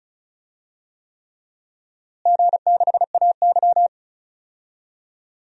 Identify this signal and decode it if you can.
Morse code, heard as a keyed tone, about 36 wpm, 700 Hz. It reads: G6AY
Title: DXCC-100 Call Signs - 4 Characters - Repeated 3x - 35wpm